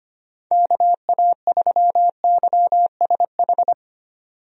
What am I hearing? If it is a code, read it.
Morse code, heard as a keyed tone, about 25 wpm, 700 Hz. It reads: KA3YSH
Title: USA Call Signs - 6 Characters - 25wpm